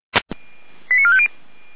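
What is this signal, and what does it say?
Five-tone selective call (ZVEI), heard as quick runs of five tones, tones of about 70 ms. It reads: 89450
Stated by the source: ZVEI five-tone selective call 08945 received by SDR.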